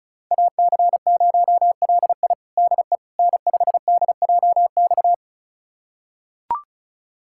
Morse code, heard as a keyed tone, about 34 wpm, 700 Hz. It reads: AC0LI DE N5DJX
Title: USA Callsign DE Callsign 35wpm